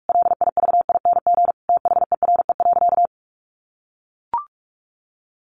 Morse code, heard as a keyed tone, about 44 wpm, 700 Hz. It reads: LIVING THERE.
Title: Sentences from Top 1500 Words 45wpm